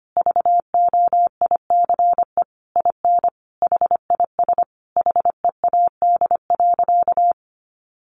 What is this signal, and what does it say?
Morse code, heard as a keyed tone, about 25 wpm, 700 Hz. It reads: VOICE IN HIS HEAD.
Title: Sentences from Top 1300 Words 25wpm